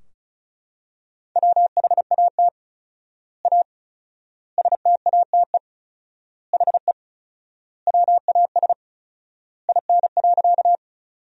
Morse code, heard as a keyed tone, about 36 wpm, 700 Hz. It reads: WHAT A STATE HE WAS IN.
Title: 4x Word-Spacing - Sentences from Top 400 Words - 35wpm